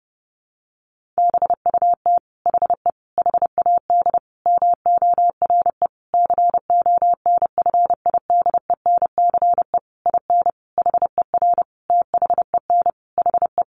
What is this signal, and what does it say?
Morse code, heard as a keyed tone, about 30 wpm, 700 Hz. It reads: BUT HE HAD MORE CONFIDENCE IN HER THEN HE
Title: Sentences from Top 1500 Words 30wpm